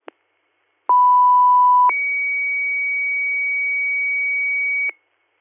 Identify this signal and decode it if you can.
Two-tone page, heard as a one-second tone, then a three-second tone, about 990 then 2250 Hz, with noise. It is heard from 0.9 to 4.9 s.